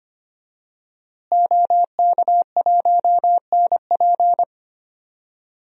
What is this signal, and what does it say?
Morse code, heard as a keyed tone, about 25 wpm, 700 Hz. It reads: OK1NP
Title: DXCC-100 Call Signs - 5 Characters - Repeated 3x - 25wpm